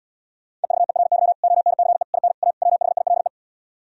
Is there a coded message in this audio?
Yes: Morse code from 0.6 to 3.3 s, heard as a keyed tone, about 77 wpm, 700 Hz. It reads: EVERY ONCE IN A WHILE